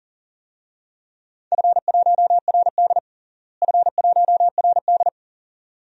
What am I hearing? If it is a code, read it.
Morse code, heard as a keyed tone, about 40 wpm, 700 Hz. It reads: F1RD F1RD
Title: DXCC-100 Call Signs - 4 Characters - Repeated 2x - 40wpm